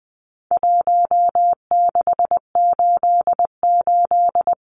Morse code, heard as a keyed tone, about 20 wpm, 700 Hz. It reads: 1688
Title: Numbers - Contest Style - 4-Digit 20wpm